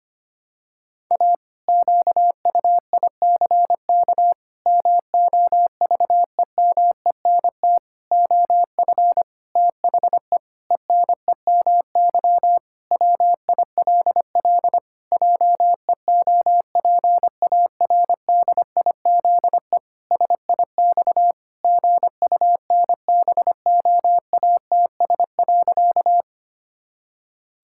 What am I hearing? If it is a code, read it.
Morse code, heard as a keyed tone, about 25 wpm, 700 Hz. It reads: A QUICK MOVEMENT OF THE ENEMY WILL JEOPARDIZE SIX GUNBOATS.
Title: Pangram Challenge v2 - 25wpm